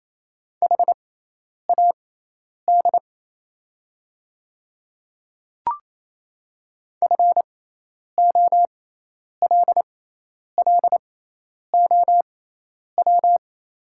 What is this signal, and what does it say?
Morse code, heard as a keyed tone, about 28 wpm, spaced as for 10 wpm, 690 Hz. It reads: HAD FOLLOW
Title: 6x Character Spacing - Top 200 Words - 28wpm